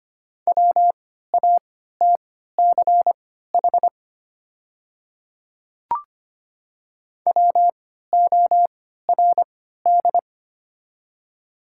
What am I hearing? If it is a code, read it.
Morse code, heard as a keyed tone, about 25 wpm, spaced as for 14 wpm, 700 Hz. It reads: WATCH WORD
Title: 3x Character Spacing - Top 500 Words - 25wpm